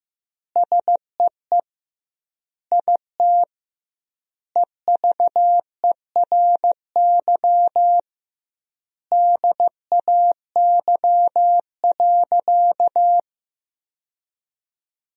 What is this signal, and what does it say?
Morse code, heard as a keyed tone, about 15 wpm, 700 Hz. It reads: SEE IT EVERY DAY.
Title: 2x Word-Spacing - Sentences from Top 200 Words - 15wpm